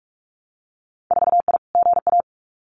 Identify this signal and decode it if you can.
Morse code, heard as a keyed tone, about 46 wpm, 700 Hz. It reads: 4I GA
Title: Winter Field Day Exchanges Encore 45wpm